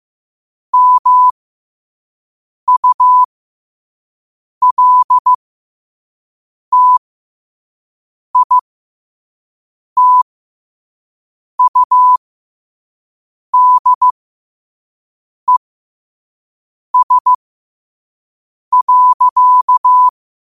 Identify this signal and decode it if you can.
Morse code, heard as a keyed tone, about 15 wpm, spaced as for 5 wpm, 1000 Hz. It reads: MULTITUDES.
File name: qotdfiles / 5wpm